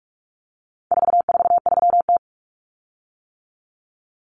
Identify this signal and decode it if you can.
Morse code, heard as a keyed tone, about 44 wpm, 700 Hz. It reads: 443T
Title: Common Cut Numbers - 4-Digit 45wpm